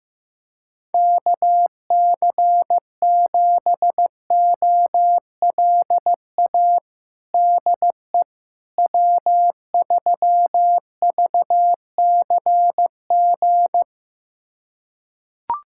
Morse code, heard as a keyed tone, about 15 wpm, 700 Hz. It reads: KC7OLA DE W3VCG
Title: USA Callsign DE Callsign 15wpm